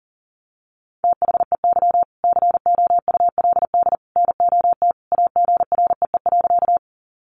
Morse code, heard as a keyed tone, about 40 wpm, 700 Hz. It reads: THEY COULD NOT AGREE.
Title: Sentences from Top 1100 Words 40wpm